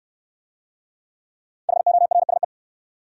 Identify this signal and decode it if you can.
Morse code, heard as a keyed tone, about 61 wpm, 700 Hz. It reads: HORSE